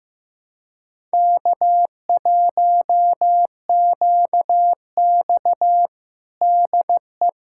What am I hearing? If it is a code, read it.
Morse code, heard as a keyed tone, about 15 wpm, 700 Hz. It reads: K1QX DE
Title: Practice QSO - Exchange 4 - Ending - 15wpm